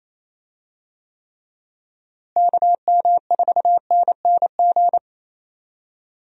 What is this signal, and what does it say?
Morse code, heard as a keyed tone, about 27 wpm, 700 Hz. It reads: KM4NNG
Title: USA Call Signs - 6 Characters - Repeated 2x - 28wpm